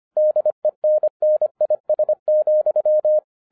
Morse code, heard as a keyed tone, about 25 wpm, 610 Hz. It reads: DENNIS,
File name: 2012-01-mp3 / n14-FUN WITH NUMBERS-25wpm